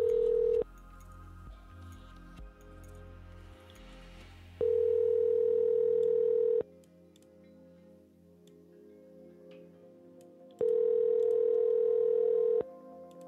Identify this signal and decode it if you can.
Ringback tone (North America), 440 + 480 Hz.